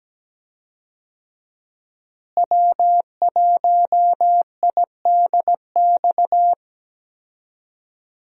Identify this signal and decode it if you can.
Morse code, heard as a keyed tone, about 17 wpm, 700 Hz. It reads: W1IDX
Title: USA Call Signs - 5 Characters - Repeated 2x - 17wpm